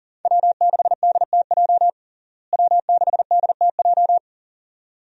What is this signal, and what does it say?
Morse code, heard as a keyed tone, about 37 wpm, 700 Hz. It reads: W6DTJ W6DTJ
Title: USA Call Signs - 5 Characters - Repeated 2x - 40wpm